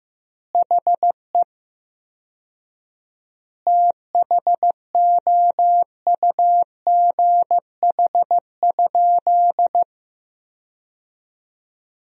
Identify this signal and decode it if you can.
Morse code, heard as a keyed tone, about 16 wpm, 700 Hz. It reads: HE THOUGH?
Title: 4x Word-Spacing - Sentences from Top 400 Words - 15wpm